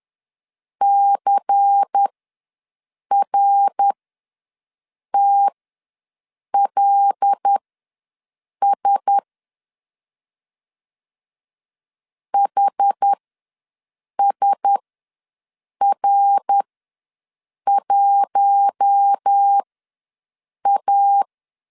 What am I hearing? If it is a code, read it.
Morse code, heard as a keyed tone, about 10 wpm, spaced as for 6 wpm, 800 Hz. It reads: CRTLS HSR1A
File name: LEZIONI / 5 LBVC / 5 LBVC ws 10wpm 3733